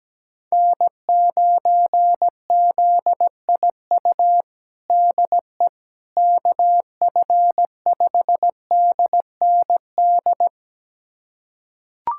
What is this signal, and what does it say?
Morse code, heard as a keyed tone, about 17 wpm, 700 Hz. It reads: N9ZIU DE KF5DND
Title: USA Callsign DE Callsign 17wpm